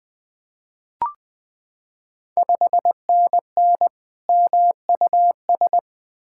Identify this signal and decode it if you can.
Morse code, heard as a keyed tone, about 20 wpm, 690 Hz. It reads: 5NN MUS
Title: Ontario QSO Party Exchange 20wpm